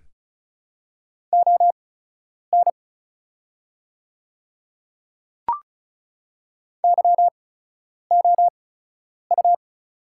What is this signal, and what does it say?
Morse code, heard as a keyed tone, about 35 wpm, spaced as for 10 wpm, 700 Hz. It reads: ON YOU
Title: 8x Character Spacing - Top 100 Words - 35wpm